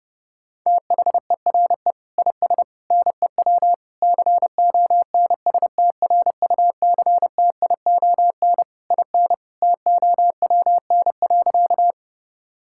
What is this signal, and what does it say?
Morse code, heard as a keyed tone, about 32 wpm, 700 Hz. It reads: THERE IS NEW CONSTRUCTION IN TOWN.
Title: Sentences from Top 1600 Words 30wpm